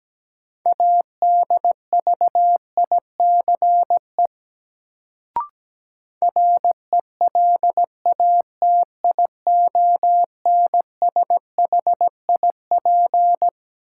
Morse code, heard as a keyed tone, about 17 wpm, 700 Hz. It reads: ADVICE RELATIONSHIP
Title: Sentences from Top 1100 Words - New Word Drill Alt - 17wpm